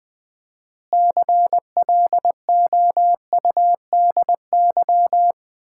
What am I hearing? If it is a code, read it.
Morse code, heard as a keyed tone, about 20 wpm, 700 Hz. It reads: CLOUDY
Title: Common QSO Phrases - 4x Word-Spacing - 20wpm